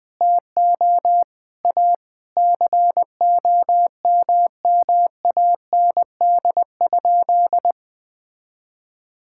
Morse code, heard as a keyed tone, about 20 wpm, 700 Hz. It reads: TO A COMMAND?
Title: Sentences from Top 1600 Words 20wpm